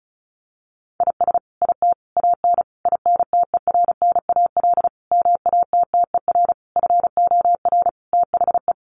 Morse code, heard as a keyed tone, about 34 wpm, 700 Hz. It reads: IS IT AN INTERNAL MATTER FOR THE
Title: Sentences from Top 1400 Words 35wpm